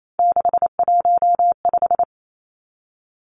Morse code, heard as a keyed tone, about 28 wpm, 700 Hz. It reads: 615
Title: Numbers - 3-Digit 28wpm